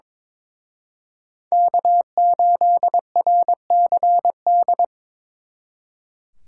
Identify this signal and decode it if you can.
Morse code, heard as a keyed tone, about 25 wpm, 700 Hz. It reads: K8RCD
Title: Speed Racing - USA 1x3 Call Signs 22wpm